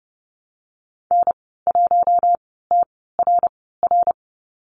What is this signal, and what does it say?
Morse code, heard as a keyed tone, about 30 wpm, spaced as for 17 wpm, 700 Hz. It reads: N1TRR
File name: Lesson 14g - Number 2 - Call Signs 30wpm